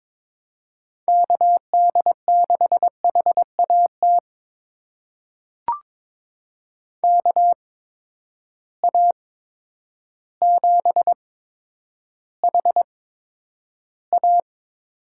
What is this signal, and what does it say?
Morse code, heard as a keyed tone, about 22 wpm, spaced as for 10 wpm, 700 Hz. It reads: KD6HAT KA7HA
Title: USA Call Signs - 6 Characters - 8x Character Spacing - 22wpm